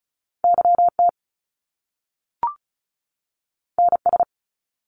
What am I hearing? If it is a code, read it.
Morse code, heard as a keyed tone, about 36 wpm, 700 Hz. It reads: YT NS
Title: Canadian Provinces and Territories 35wpm